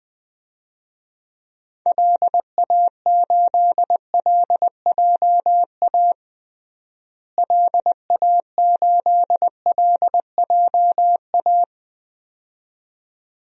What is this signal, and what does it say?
Morse code, heard as a keyed tone, about 21 wpm, 700 Hz. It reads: LA8LJA LA8LJA
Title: DXCC-100 Call Signs - 6 Characters - Repeated 2x - 20wpm